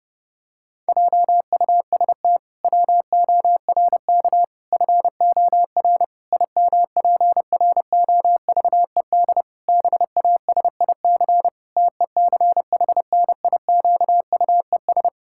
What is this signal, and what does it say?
Morse code, heard as a keyed tone, about 30 wpm, 700 Hz. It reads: JUST WORK FOR IMPROVED BASIC TECHNIQUES
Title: Pangram Challenge v2 - 30wpm